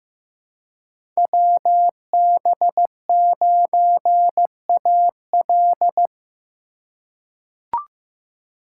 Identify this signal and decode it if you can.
Morse code, heard as a keyed tone, about 15 wpm, 700 Hz. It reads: WB9AL
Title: USA Call Signs - 5 Characters - 2x Character Spacing - 15wpm